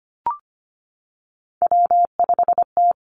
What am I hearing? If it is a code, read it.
Morse code, heard as a keyed tone, about 25 wpm, 700 Hz. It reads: W5T
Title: Three Challenging Letter-Number Combinations 25wpm